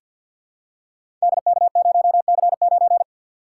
Morse code, heard as a keyed tone, about 49 wpm, 700 Hz. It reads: DK0C9
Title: Code Groups - 5 Letter:Numbers - 50wpm